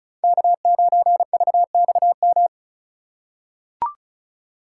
Morse code, heard as a keyed tone, about 36 wpm, 700 Hz. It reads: K9VXM
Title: USA Call Signs - 5 Characters - 3x Character Spacing - 35wpm